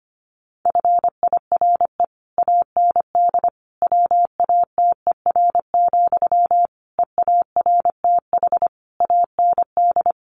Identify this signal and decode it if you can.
Morse code, heard as a keyed tone, about 25 wpm, 700 Hz. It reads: FIRE AND WATER, EARTH AND
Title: Sentences from Top 400 Words 25wpm